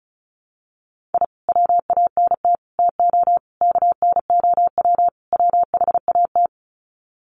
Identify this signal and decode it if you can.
Morse code, heard as a keyed tone, about 34 wpm, 700 Hz. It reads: I WANT TO KNOW WHAT
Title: Sets of 5 Words Encore2 35wpm